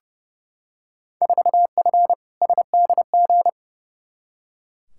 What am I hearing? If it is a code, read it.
Morse code, heard as a keyed tone, about 30 wpm, 700 Hz. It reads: 4F SDG
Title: Field Day Exchanges - 30wpm